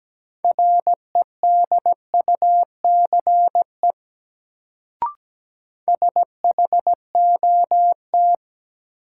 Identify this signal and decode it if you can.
Morse code, heard as a keyed tone, about 17 wpm, 700 Hz. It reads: REDUCE SHOT
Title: Sentences from Top 1100 Words - New Word Drill Alt - 17wpm